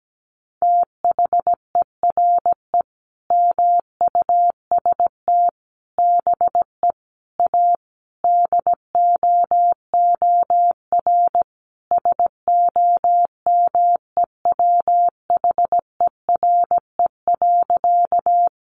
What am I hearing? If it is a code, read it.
Morse code, heard as a keyed tone, about 17 wpm, 700 Hz. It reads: THERE MUST BE A DOOR SOMEWHERE.
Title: Sentences from Top 1400 Words 17wpm